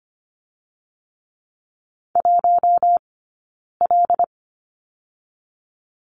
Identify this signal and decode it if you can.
Morse code, heard as a keyed tone, about 25 wpm, 700 Hz. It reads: 1 L
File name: Lesson 15d - Number 5 - Two Characters 25wpm